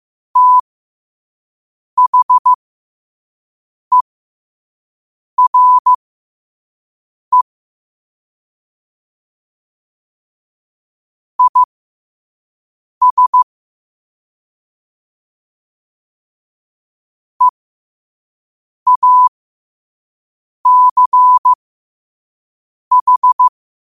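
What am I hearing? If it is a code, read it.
Morse code, heard as a keyed tone, about 15 wpm, spaced as for 5 wpm, 1000 Hz. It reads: THERE IS EACH
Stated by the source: Morse code quote of the day sent at 5 wpm.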